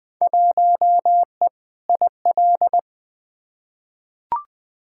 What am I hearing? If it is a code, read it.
Morse code, heard as a keyed tone, about 20 wpm, 700 Hz. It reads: 1E IL
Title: Field Day Exchanges - 20wpm